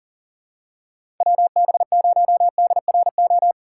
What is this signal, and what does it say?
Morse code, heard as a keyed tone, about 40 wpm, 700 Hz. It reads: WB0DRO
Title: USA Call Signs - 6 Characters - 6x Character Spacing - 40wpm